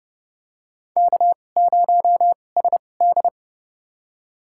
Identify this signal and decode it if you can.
Morse code, heard as a keyed tone, about 30 wpm, spaced as for 22 wpm, 700 Hz. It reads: K0SD
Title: USA Call Signs - 4 Characters - 2x Character Spacing - 30wpm